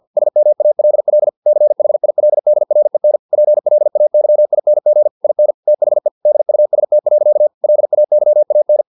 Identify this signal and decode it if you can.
Morse code, heard as a keyed tone, about 50 wpm, 600 Hz. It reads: SMALL CHILDREN PLAYING IN THE DUST. LAYAN
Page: Never 49wpm